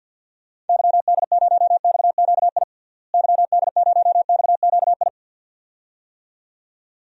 Morse code, heard as a keyed tone, about 51 wpm, 700 Hz. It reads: YD0XQI YD0XQI
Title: DXCC-100 Call Signs - 6 Characters - Repeated 2x - 50wpm